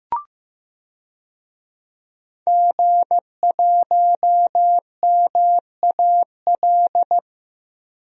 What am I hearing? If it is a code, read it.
Morse code, heard as a keyed tone, about 15 wpm, 700 Hz. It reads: G1MAL
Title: DXCC-100 Call Signs - 5 Characters - Repeated 3x - 15wpm